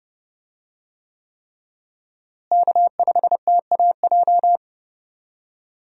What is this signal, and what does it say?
Morse code, heard as a keyed tone, about 32 wpm, 700 Hz. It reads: K5TAJ
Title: USA Call Signs - 5 Characters - Repeated 3x - 30wpm